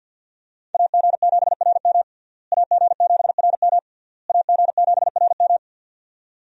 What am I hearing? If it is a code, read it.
Morse code, heard as a keyed tone, about 50 wpm, 700 Hz. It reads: AG7RM AG7RM AG7RM
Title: USA Call Signs - 5 Characters - Repeated 3x - 50wpm